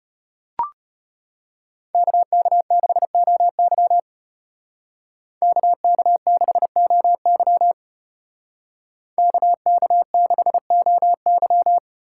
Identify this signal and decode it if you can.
Morse code, heard as a keyed tone, about 34 wpm, 700 Hz. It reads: KK6OY KK6OY KK6OY